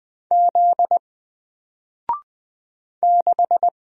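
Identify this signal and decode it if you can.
Morse code, heard as a keyed tone, about 20 wpm, 700 Hz. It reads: Z 6
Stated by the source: Single letter and number Morse drill at 20 wpm, from N4LCD.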